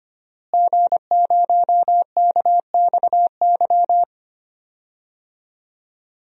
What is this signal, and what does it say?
Morse code, heard as a keyed tone, about 25 wpm, 700 Hz. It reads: G0KXY
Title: DXCC-100 Call Signs - 5 Characters - Repeated 2x - 25wpm